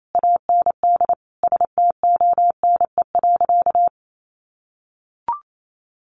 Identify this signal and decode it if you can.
Morse code, heard as a keyed tone, about 28 wpm, 700 Hz. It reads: AND STONE.
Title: A Sound Of Thunder 28wpm